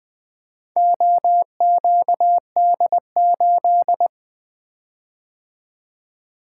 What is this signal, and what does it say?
Morse code, heard as a keyed tone, about 23 wpm, 700 Hz. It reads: OQD8
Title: Speed-Racing - Four Letter-Number Combinations - 20wpm